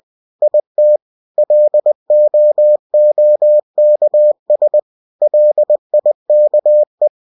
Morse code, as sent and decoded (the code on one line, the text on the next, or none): .. - / .-.. --- --- -.- ... / .-.. .. -.- .
IT LOOKS LIKE